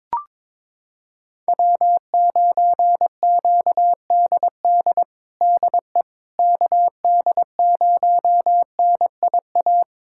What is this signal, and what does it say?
Morse code, heard as a keyed tone, about 22 wpm, 700 Hz. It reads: W9QDD DE KD0NIA
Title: USA Callsign DE Callsign 22wpm